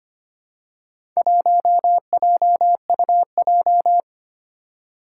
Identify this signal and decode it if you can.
Morse code, heard as a keyed tone, about 25 wpm, 700 Hz. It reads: 1JUJ
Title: Four Challenging Letter-Number Combinations 25wpm